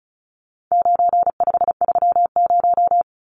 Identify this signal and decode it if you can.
Morse code, heard as a keyed tone, about 34 wpm, 700 Hz. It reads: 9530
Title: Numbers - 4-Digit 35wpm